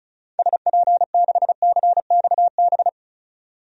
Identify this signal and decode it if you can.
Morse code, heard as a keyed tone, about 36 wpm, 700 Hz. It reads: SP6CXB
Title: DXCC-100 Call Signs - 6 Characters - 35wpm